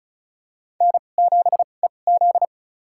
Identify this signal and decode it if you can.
Morse code, heard as a keyed tone, about 34 wpm, spaced as for 25 wpm, 700 Hz. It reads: N7EZ
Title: USA Call Signs - 4 Characters - 2x Character Spacing - 35wpm